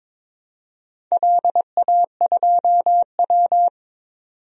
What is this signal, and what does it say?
Morse code, heard as a keyed tone, about 22 wpm, 700 Hz. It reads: LA2W
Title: DXCC-100 Call Signs - 4 Characters - 22wpm